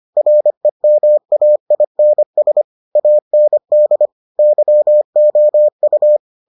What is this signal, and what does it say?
Morse code, heard as a keyed tone, about 25 wpm, 600 Hz. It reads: REMAINS AND YOU